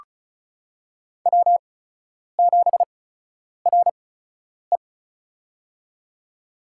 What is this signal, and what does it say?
Morse code, heard as a keyed tone, about 36 wpm, spaced as for 10 wpm, 700 Hz. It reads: W7RE
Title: USA Call Signs - 8x Character Spacing - 35wpm